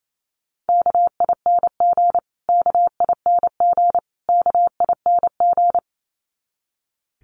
Morse code, heard as a keyed tone, about 28 wpm, 700 Hz. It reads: KING KING KING
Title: Top 500 Words Repeated 28wpm